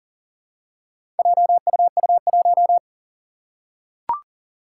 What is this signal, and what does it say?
Morse code, heard as a keyed tone, about 41 wpm, 700 Hz. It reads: JUU1
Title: Four Challenging Letter-Number Combinations 40wpm